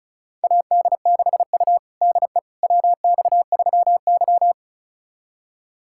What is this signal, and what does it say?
Morse code, heard as a keyed tone, about 36 wpm, 700 Hz. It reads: AD6U DE WX3Y
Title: USA Callsign DE Callsign - 4 Characters - 35wpm